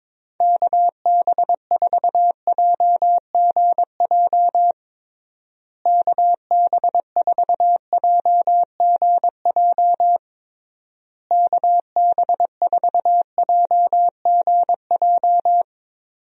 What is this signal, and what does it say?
Morse code, heard as a keyed tone, about 22 wpm, 700 Hz. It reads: KB4JGJ KB4JGJ KB4JGJ
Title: USA Call Signs - 6 Characters - Repeated 3x - 22wpm